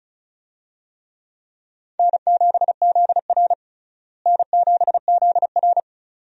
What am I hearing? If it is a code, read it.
Morse code, heard as a keyed tone, about 36 wpm, 700 Hz. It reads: N7ZR N7ZR
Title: USA Call Signs - 4 Characters - Repeated 2x - 35wpm